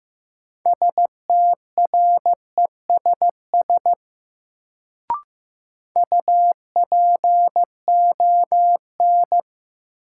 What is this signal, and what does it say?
Morse code, heard as a keyed tone, about 15 wpm, 700 Hz. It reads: STRESS UPON
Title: Sentences from Top 1000 Words - Review Part 6 - New Word Drill - 15wpm